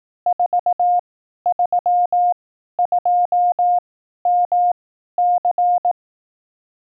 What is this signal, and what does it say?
Morse code, heard as a keyed tone, about 18 wpm, spaced as for 12 wpm, 710 Hz. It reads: 432MC
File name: 12WPM test